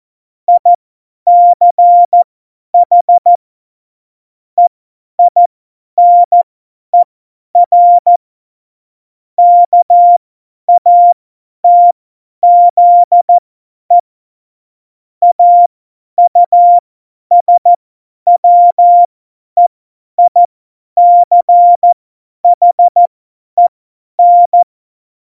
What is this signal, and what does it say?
Morse code, heard as a keyed tone, about 14 wpm, spaced as for 10 wpm, 700 Hz. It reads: ICH EINER KATZE AUSWEICHEN
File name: Morsekurs / Witze / Witz 29. September 2020, 50 F70